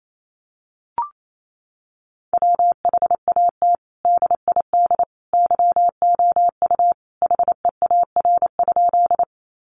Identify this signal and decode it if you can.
Morse code, heard as a keyed tone, about 28 wpm, 700 Hz. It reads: WHAT DID YOU HEAR?
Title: Sentences from Top 300 Words 28wpm